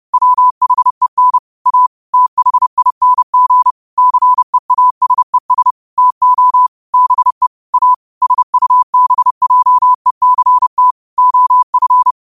Morse code, heard as a keyed tone, about 30 wpm, 1000 Hz. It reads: WHEN A THING CEASES TO BE A SUBJECT OF